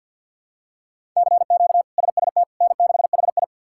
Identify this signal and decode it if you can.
Morse code, heard as a keyed tone, about 53 wpm, 700 Hz. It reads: CQ SST N6HI
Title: K1USN SST QSO Practice - 50wpm